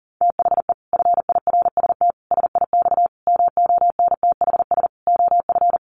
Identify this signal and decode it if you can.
Morse code, heard as a keyed tone, about 40 wpm, 700 Hz. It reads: THE FIRST SIX MONTHS OF
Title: Sets of 5 Words Encore2 40wpm